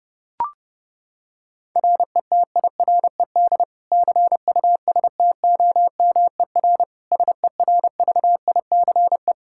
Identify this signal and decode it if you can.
Morse code, heard as a keyed tone, about 30 wpm, 700 Hz. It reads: RETIRED CUSTOMER SERVICE
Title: Practice QSO - Exchange 3 - More Info - 30wpm